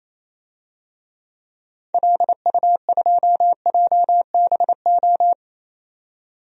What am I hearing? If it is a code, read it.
Morse code, heard as a keyed tone, about 28 wpm, 700 Hz. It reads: LU2JBO
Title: DXCC-100 Call Signs - 6 Characters - 28wpm